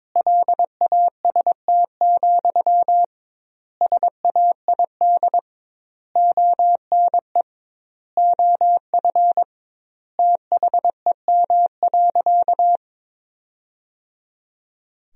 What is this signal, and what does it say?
Morse code, heard as a keyed tone, about 22 wpm, 700 Hz. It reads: LAST, SAID ONE OF THEM.
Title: 2x Word-Spacing - Sentences from Top 300 Words - 22wpm